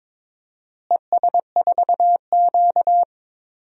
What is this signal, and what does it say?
Morse code, heard as a keyed tone, about 22 wpm, 700 Hz. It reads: ES4Q
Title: DXCC-100 Call Signs - 4 Characters - Repeated 2x - 22wpm